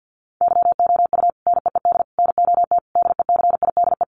Morse code, heard as a keyed tone, about 50 wpm, 700 Hz. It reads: YOU NEED NOT DECIDE
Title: Sentences from Top 1500 Words 50wpm